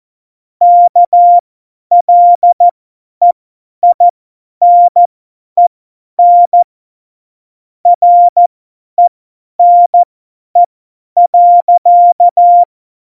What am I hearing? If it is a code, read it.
Morse code, heard as a keyed tone, about 14 wpm, spaced as for 10 wpm, 700 Hz. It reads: KLEINEN RENE.
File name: Morsekurs / Witze / Witz 20. Februar 2018, 50 F70